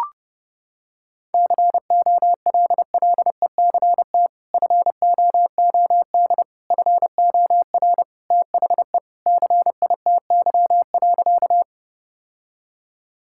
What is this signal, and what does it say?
Morse code, heard as a keyed tone, about 32 wpm, 700 Hz. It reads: COLLECT FOOD FOR THE CITY.
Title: Sentences from Top 1600 Words 30wpm